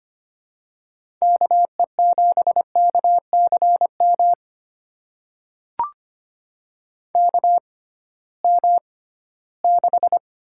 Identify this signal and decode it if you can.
Morse code, heard as a keyed tone, about 25 wpm, spaced as for 16 wpm, 700 Hz. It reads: KE7KCM KM6
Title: USA Call Signs - 6 Characters - 6x Character Spacing - 25wpm